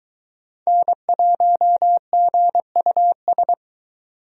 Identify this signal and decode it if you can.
Morse code, heard as a keyed tone, about 23 wpm, 700 Hz. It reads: N1GUS